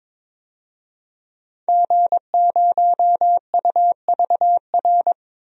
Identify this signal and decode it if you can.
Morse code, heard as a keyed tone, about 21 wpm, 700 Hz. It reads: G0UVR
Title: DXCC-100 Call Signs - 5 Characters - Repeated 3x - 22wpm